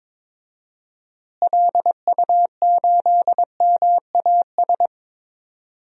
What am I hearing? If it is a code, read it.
Morse code, heard as a keyed tone, about 22 wpm, 700 Hz. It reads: LU8MAS
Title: DXCC-100 Call Signs - 6 Characters - Repeated 2x - 22wpm